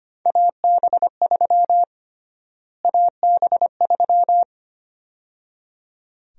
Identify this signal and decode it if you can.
Morse code, heard as a keyed tone, about 25 wpm, 700 Hz. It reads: AB3 AB3
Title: USA Call Sign Prefixes - Repeated 2x - 25wpm